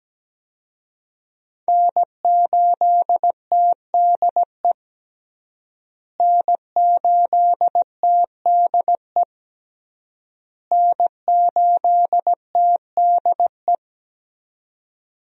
Morse code, heard as a keyed tone, about 17 wpm, 700 Hz. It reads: N8TDE N8TDE N8TDE
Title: USA Call Signs - 5 Characters - Repeated 3x - 17wpm